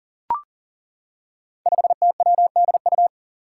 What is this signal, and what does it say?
Morse code, heard as a keyed tone, about 40 wpm, 700 Hz. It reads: 5TWDU
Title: Five Challenging Letter-Number Combinations 40wpm